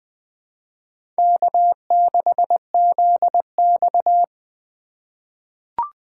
Morse code, heard as a keyed tone, about 21 wpm, 700 Hz. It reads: K6ZX
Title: K1USN SST QSO Practice - 20wpm